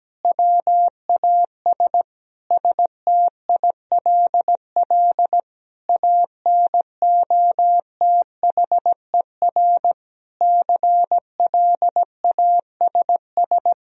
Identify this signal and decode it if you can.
Morse code, heard as a keyed tone, about 17 wpm, 700 Hz. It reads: WAS STILL ANOTHER CLASS
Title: Sentences from Top 1000 Words - Review Part 2 - 17wpm